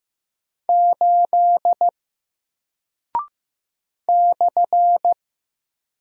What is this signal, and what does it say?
Morse code, heard as a keyed tone, about 15 wpm, 700 Hz. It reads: 8 /
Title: Single Character (Letter:Number:Punctuation:Prosign) 15wpm